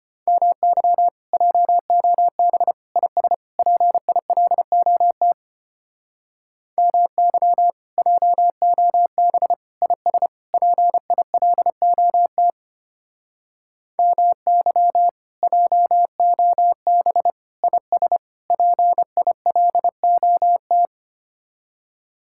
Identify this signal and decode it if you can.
Morse code, heard as a keyed tone, about 32 wpm, 700 Hz. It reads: MY JOB IS PILOT MY JOB IS PILOT MY JOB IS PILOT